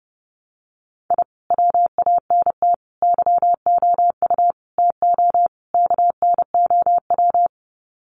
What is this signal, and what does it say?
Morse code, heard as a keyed tone, about 30 wpm, 700 Hz. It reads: I WANT YOU TO KNOW
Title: Sets of 5 Words Encore 30wpm